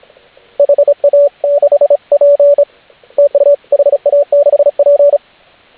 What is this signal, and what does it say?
Morse code, heard as a keyed tone, about 33 wpm, 570 Hz, with radio noise. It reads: HA6P TU HA6P